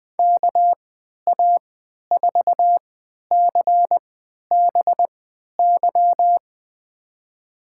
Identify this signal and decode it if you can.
Morse code, heard as a keyed tone, about 20 wpm, spaced as for 11 wpm, 700 Hz. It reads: KA4CBY
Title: USA Call Signs - 6 Characters - 3x Character Spacing - 20wpm